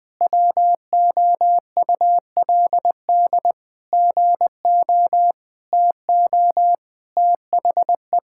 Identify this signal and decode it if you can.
Morse code, heard as a keyed tone, about 20 wpm, 700 Hz. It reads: WOULD GO TO THE
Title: Sentences from Top 1600 Words 20wpm